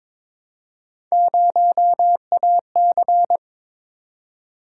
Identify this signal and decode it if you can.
Morse code, heard as a keyed tone, about 22 wpm, 700 Hz. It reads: 0AC
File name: practice / 22wpm / Three Letter-Number Combinations - 22wpm